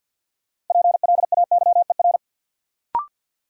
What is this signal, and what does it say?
Morse code, heard as a keyed tone, about 48 wpm, 700 Hz. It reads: PLAYER
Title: Sentences from Top 1000 Words - Review Part 4 - New Word Drill - 50wpm